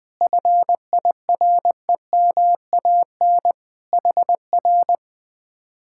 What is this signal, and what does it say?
Morse code, heard as a keyed tone, about 20 wpm, 700 Hz. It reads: FIREMAN HR
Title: Speed-Racing Common QSO Phrases - 17wpm